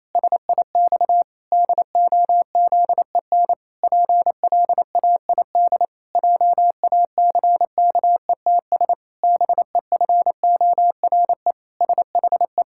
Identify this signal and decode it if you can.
Morse code, heard as a keyed tone, about 28 wpm, 700 Hz. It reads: SIX DOZEN PLAID JACKETS BEFORE SHE
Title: Pangram Challenge v2 - 28wpm